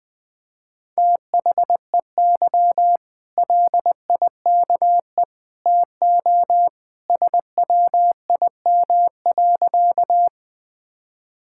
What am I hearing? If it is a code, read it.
Morse code, heard as a keyed tone, about 20 wpm, 700 Hz. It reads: THEY LIKE TO SWIM.
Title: Easy Sentences with Hints - 20wpm